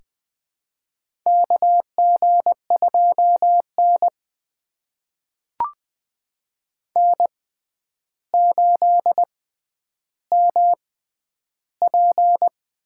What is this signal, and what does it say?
Morse code, heard as a keyed tone, about 20 wpm, spaced as for 10 wpm, 700 Hz. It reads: KG2N N8MP
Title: USA Call Signs - 4 Characters - 6x Character Spacing - 20wpm